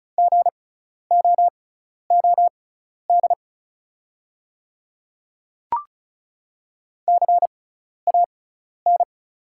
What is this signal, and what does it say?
Morse code, heard as a keyed tone, about 36 wpm, spaced as for 12 wpm, 690 Hz. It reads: GOOD CAN
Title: 6x Character Spacing - Top 300 Words - 35wpm